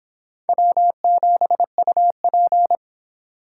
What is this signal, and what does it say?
Morse code, heard as a keyed tone, about 30 wpm, 700 Hz. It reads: W7UP